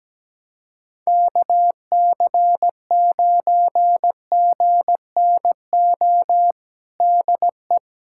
Morse code, heard as a keyed tone, about 17 wpm, 700 Hz. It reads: KC9GNO DE
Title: USA Callsign DE Callsign - 6 Characters - 17wpm